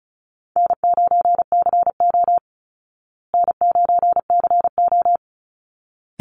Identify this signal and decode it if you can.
Morse code, heard as a keyed tone, about 34 wpm, 700 Hz. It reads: N9CO N9CO
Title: Easy Call Signs Repeated 2x - 35wpm